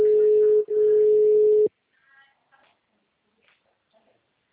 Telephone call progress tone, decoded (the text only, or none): ringback tone